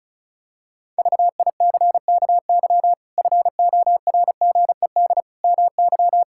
Morse code, heard as a keyed tone, about 36 wpm, 700 Hz. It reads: VICKY FORGED MY
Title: Pangram Challenge v2 - 35wpm